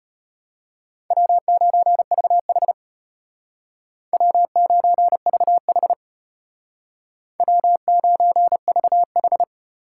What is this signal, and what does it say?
Morse code, heard as a keyed tone, about 33 wpm, 700 Hz. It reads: W9VH W9VH W9VH